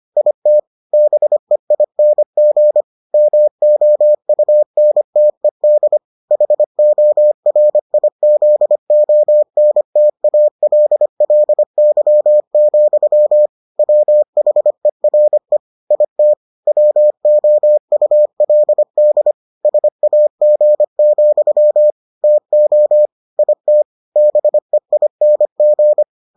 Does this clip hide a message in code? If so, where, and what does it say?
Yes: Morse code from 0.2 to 26.0 s, heard as a keyed tone, about 25 wpm, 600 Hz. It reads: IT BEING MOUNTED HORIZONTALLY, WHERE IT WOULD SAG, TO IT BEING